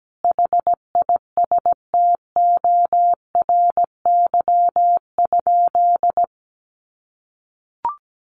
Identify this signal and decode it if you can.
Morse code, heard as a keyed tone, about 17 wpm, 700 Hz. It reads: HISTORY?
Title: Sentences from Top 1500 Words 17wpm